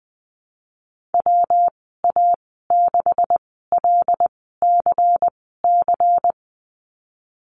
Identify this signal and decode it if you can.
Morse code, heard as a keyed tone, about 20 wpm, spaced as for 14 wpm, 700 Hz. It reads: WA6LCC
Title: 179 Lesson 29g - Slash - Call Signs 20wpm